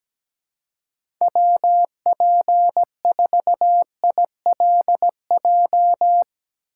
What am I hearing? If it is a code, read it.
Morse code, heard as a keyed tone, about 17 wpm, 700 Hz. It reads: WP4ILJ
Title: USA Call Signs - 4x Character Spacing - 17wpm